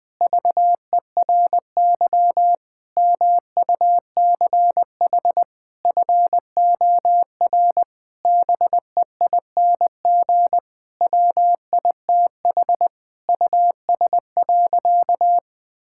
Morse code, heard as a keyed tone, about 20 wpm, 700 Hz. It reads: VERY MUCH FOR BEING WITH US.
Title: Sentences with 5-Word Phrases - 20wpm